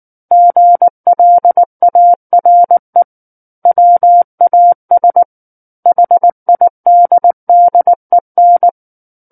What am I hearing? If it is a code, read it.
Morse code, heard as a keyed tone, about 19 wpm, 700 Hz. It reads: GLARE WAS HIDDEN
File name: Book One / 10-Chapter 10 - 19 WPM